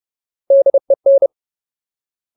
Morse code, heard as a keyed tone, about 30 wpm, 550 Hz. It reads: DEN
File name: Ultimate Word List / 30WPM Ultimate